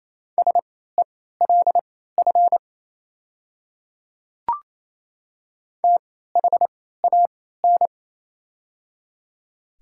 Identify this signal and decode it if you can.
Morse code, heard as a keyed tone, about 28 wpm, spaced as for 16 wpm, 700 Hz. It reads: SELF THAN
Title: 3x Character Spacing - Top 200 Words - 28wpm